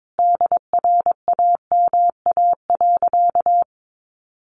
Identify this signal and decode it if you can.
Morse code, heard as a keyed tone, about 22 wpm, 700 Hz. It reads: DRAMA.
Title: Sentences from Top 1500 Words 22wpm